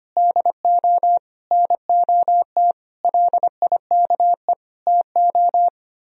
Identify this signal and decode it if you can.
Morse code, heard as a keyed tone, about 31 wpm, 700 Hz. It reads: DO NOT LIKE TO
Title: Speed-Racing - Sets of 5 Words 25wpm